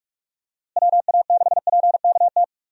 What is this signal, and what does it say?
Morse code, heard as a keyed tone, about 44 wpm, 700 Hz. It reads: WA6PKT
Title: USA Call Signs - 6 Characters - Repeated 2x - 45wpm